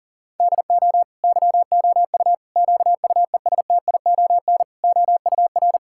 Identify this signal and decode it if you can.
Morse code, heard as a keyed tone, about 37 wpm, 700 Hz. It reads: DO YOU QUESTION OUR
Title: Sentences from Top 1000 Words - Review Part 5 - 40wpm